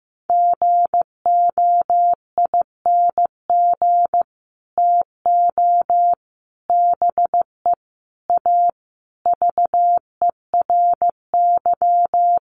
Morse code, heard as a keyed tone, about 15 wpm, 700 Hz. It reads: GOING TO BE A VERY
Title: Sets of 5 Words Encore 15wpm